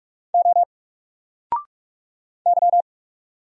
Morse code, heard as a keyed tone, about 46 wpm, 700 Hz. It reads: O Y
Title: Single Letter-Number 45wpm